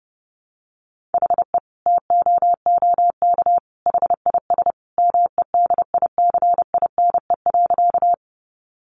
Morse code, heard as a keyed tone, about 30 wpm, 700 Hz. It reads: HE TOOK HIS MEDICINE.
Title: Sentences from Top 900 Words 30wpm